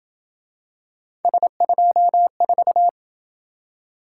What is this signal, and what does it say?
Morse code, heard as a keyed tone, about 24 wpm, 700 Hz. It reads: S24
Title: Speed-Racing - Three Letter-Number Combinations - 20wpm